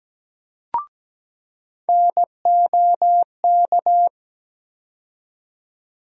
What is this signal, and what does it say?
Morse code, heard as a keyed tone, about 17 wpm, 700 Hz. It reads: NOK
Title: USA Call Sign Postfixes - 3 Letters 17wpm